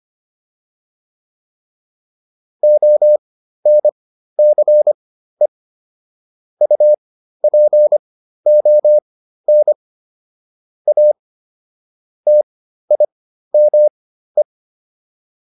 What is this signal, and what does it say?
Morse code, heard as a keyed tone, about 25 wpm, spaced as for 13 wpm, 600 Hz. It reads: ONCE UPON A TIME